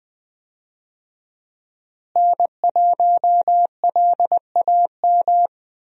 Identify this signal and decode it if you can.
Morse code, heard as a keyed tone, about 21 wpm, 700 Hz. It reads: N1LAM
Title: USA Call Signs - 5 Characters - Repeated 3x - 20wpm